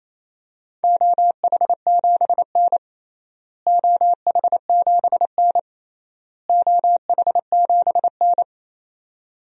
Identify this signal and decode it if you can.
Morse code, heard as a keyed tone, about 27 wpm, 700 Hz. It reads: OH7N OH7N OH7N
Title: DXCC-100 Call Signs - 4 Characters - Repeated 3x - 28wpm